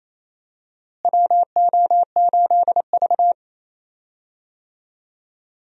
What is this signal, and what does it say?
Morse code, heard as a keyed tone, about 28 wpm, 700 Hz. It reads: WO8V
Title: USA Call Signs - 4 Characters - Repeated 2x - 28wpm